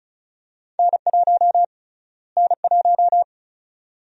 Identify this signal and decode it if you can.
Morse code, heard as a keyed tone, about 34 wpm, 700 Hz. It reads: N1 N1
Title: USA Call Sign Prefixes - Repeated 2x - 35wpm